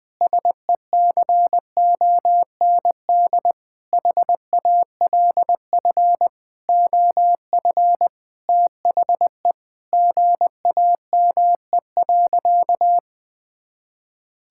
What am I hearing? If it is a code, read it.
Morse code, heard as a keyed tone, about 21 wpm, 700 Hz. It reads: SECOND HALF OF THE GAME.
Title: Sentences with 5-Word Phrases - 20wpm